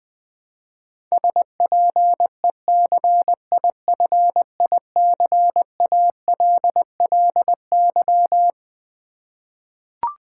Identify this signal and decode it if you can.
Morse code, heard as a keyed tone, about 21 wpm, 700 Hz. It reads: SPECIFICALLY
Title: Sentences from Top 1100 Words - New Word Drill v2 - 20wpm